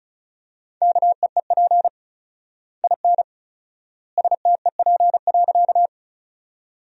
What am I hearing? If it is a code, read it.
Morse code, heard as a keyed tone, about 36 wpm, 700 Hz. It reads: KEEP IN STEP.
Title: 4x Word-Spacing - Sentences from Top 400 Words - 35wpm